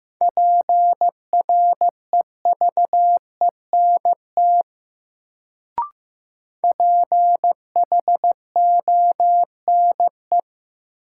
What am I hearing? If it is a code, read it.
Morse code, heard as a keyed tone, about 15 wpm, 700 Hz. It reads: PREVENT PHONE
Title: Sentences from Top 1100 Words - New Word Drill - 15wpm